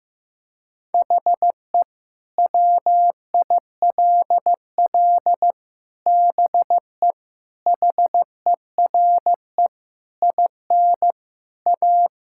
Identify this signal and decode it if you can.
Morse code, heard as a keyed tone, about 15 wpm, 700 Hz. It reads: HE WILL BE HERE IN A
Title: Sentences from Top 1000 Words - Review Part 5 - 15wpm